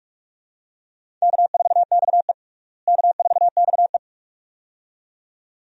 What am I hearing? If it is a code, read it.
Morse code, heard as a keyed tone, about 46 wpm, 700 Hz. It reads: K4XE K4XE
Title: USA Call Signs - 4 Characters - Repeated 2x - 45wpm